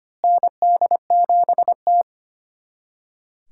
Morse code, heard as a keyed tone, about 25 wpm, 700 Hz. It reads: ND7T
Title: USA Callsign DE Callsign - 4 Characters - 25wpm